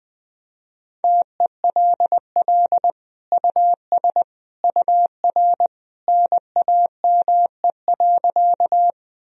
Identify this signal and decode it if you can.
Morse code, heard as a keyed tone, about 20 wpm, 700 Hz. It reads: TELL US UR NAME.
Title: Sentences from Top 200 Words and Abbreviations 20wpm